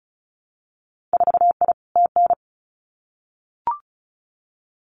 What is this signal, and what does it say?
Morse code, heard as a keyed tone, about 36 wpm, 700 Hz. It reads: 4I TN
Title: Winter Field Day Exchanges 35wpm